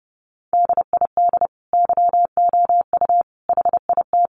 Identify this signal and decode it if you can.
Morse code, heard as a keyed tone, about 30 wpm, 700 Hz. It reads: DID YOU HIT
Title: Sentences from Top 1200 Words 30wpm